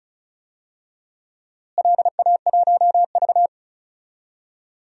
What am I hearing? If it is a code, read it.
Morse code, heard as a keyed tone, about 36 wpm, 700 Hz. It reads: LA1V
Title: DXCC-100 Call Signs - 4 Characters - Repeated 2x - 35wpm